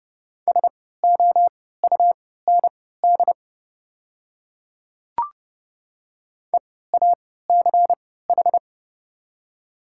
Morse code, heard as a keyed tone, about 30 wpm, spaced as for 17 wpm, 690 Hz. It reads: SOUND EACH
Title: 3x Character Spacing - Top 100 Words - 30wpm